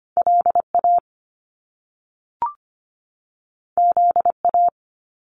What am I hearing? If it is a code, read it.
Morse code, heard as a keyed tone, about 25 wpm, 700 Hz. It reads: LA ZA
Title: Top Two Letter Combinations in Words 25wpm